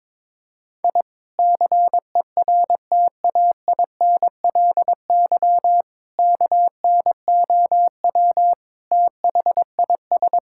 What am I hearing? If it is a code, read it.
Morse code, heard as a keyed tone, about 22 wpm, 700 Hz. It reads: I CERTAINLY KNOW THIS
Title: Sentences from Top 1000 Words - Review Part 6 - 22wpm